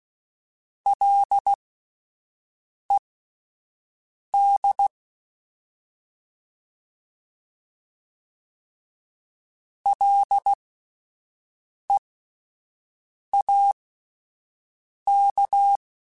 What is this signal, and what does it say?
Morse code, heard as a keyed tone, about 16 wpm, spaced as for 5 wpm, 800 Hz. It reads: LED LEAK
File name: Alphabetical / Alphabetical mp3s / Ls @5